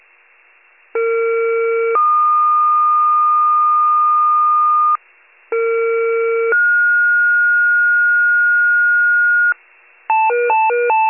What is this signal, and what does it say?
Two-tone page, heard as a one-second tone, then a three-second tone, about 460 then 1220 Hz, about 460 then 1430 Hz, with noise. It is heard from 0.9 to 9.5 s.